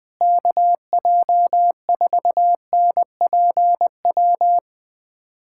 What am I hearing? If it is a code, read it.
Morse code, heard as a keyed tone, about 20 wpm, 700 Hz. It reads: KJ4NPW
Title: USA Callsign DE Callsign - 6 Characters - 20wpm